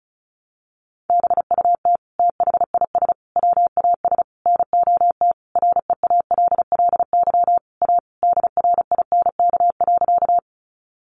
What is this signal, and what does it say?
Morse code, heard as a keyed tone, about 34 wpm, 700 Hz. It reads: BUT THIS WAS NOT REALLY A DRINK.
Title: Sentences from Top 1500 Words 35wpm